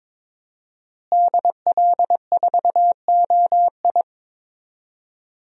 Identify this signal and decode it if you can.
Morse code, heard as a keyed tone, about 22 wpm, 700 Hz. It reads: DL4OI
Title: USA Call Signs - 5 Characters - Repeated 2x - 22wpm